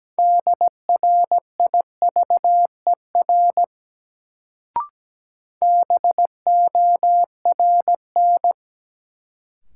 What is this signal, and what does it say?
Morse code, heard as a keyed tone, about 17 wpm, 700 Hz. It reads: DRIVER BORN
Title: Sentences from Top 1000 Words - New Word Drill - 17wpm